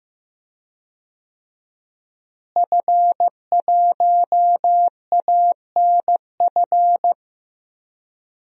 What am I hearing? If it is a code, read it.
Morse code, heard as a keyed tone, about 15 wpm, 700 Hz. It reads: F1ANF
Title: DXCC-100 Call Signs - 5 Characters - Repeated 3x - 15wpm